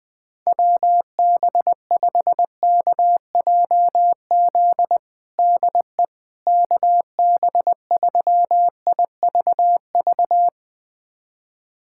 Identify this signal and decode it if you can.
Morse code, heard as a keyed tone, about 21 wpm, 700 Hz. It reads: WB5KJZ DE KB3IVV
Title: USA Callsign DE Callsign - 6 Characters - 20wpm